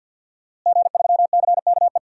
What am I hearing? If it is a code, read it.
Morse code, heard as a keyed tone, about 47 wpm, 700 Hz. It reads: G3CKE
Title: DXCC-100 Call Signs - 5 Characters - Repeated 2x - 50wpm